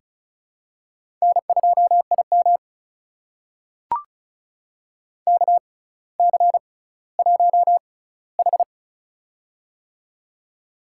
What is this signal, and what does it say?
Morse code, heard as a keyed tone, about 36 wpm, spaced as for 18 wpm, 690 Hz. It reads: N2IM KC1H
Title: USA Call Signs - 4 Characters - 6x Character Spacing - 35wpm